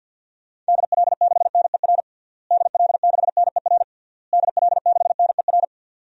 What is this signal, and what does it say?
Morse code, heard as a keyed tone, about 45 wpm, 700 Hz. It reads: DL6NER DL6NER DL6NER
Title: DXCC-100 Call Signs - 6 Characters - Repeated 3x - 50wpm